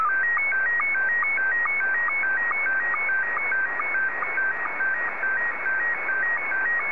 SSTV (slow-scan television), Scottie 1.
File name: sstv / S1-VK2ELO